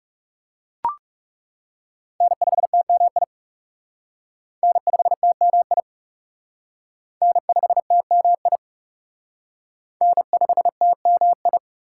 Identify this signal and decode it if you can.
Morse code, heard as a keyed tone, about 38 wpm, 690 Hz. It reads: N5TMI N5TMI N5TMI N5TMI